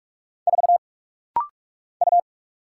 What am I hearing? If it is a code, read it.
Morse code, heard as a keyed tone, about 44 wpm, 700 Hz. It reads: 4 U
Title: Single Letter-Number - Rapid-Fire 45wpm